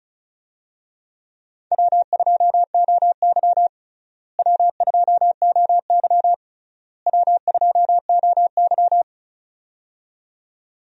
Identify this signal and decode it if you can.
Morse code, heard as a keyed tone, about 36 wpm, 700 Hz. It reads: W2OY W2OY W2OY
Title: USA Call Signs - 4 Characters - Repeated 3x - 35wpm